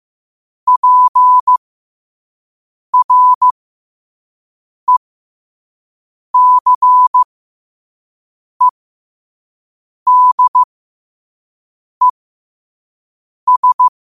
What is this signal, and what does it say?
Morse code, heard as a keyed tone, about 15 wpm, spaced as for 5 wpm, 1000 Hz. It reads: PRECEDES
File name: qotdfiles / 5wpm